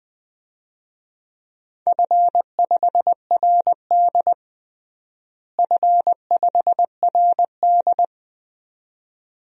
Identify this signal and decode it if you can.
Morse code, heard as a keyed tone, about 20 wpm, 700 Hz. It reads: F5RD F5RD
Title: DXCC-100 Call Signs - 4 Characters - Repeated 2x - 20wpm